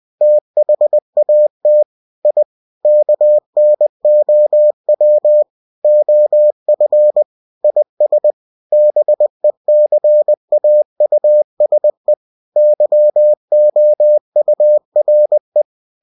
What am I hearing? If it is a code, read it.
Morse code, heard as a keyed tone, about 20 wpm, 600 Hz. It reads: THAT I KNOW OF IS BECAUSE YOURE
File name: cw-books / winnie-the-pooh / winnie-20wpm-page-0002